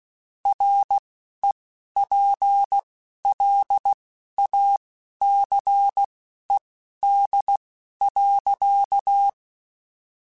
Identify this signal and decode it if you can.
Morse code, heard as a keyed tone, about 16 wpm, spaced as for 12 wpm, 800 Hz. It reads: REPLACED.
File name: Spaced / Chuckles Spaced-mp3s / Squeaky sp @15